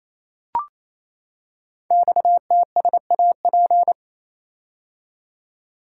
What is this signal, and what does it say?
Morse code, heard as a keyed tone, about 27 wpm, 700 Hz. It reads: XTSAP
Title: Code Groups - 5 Letters - 28wpm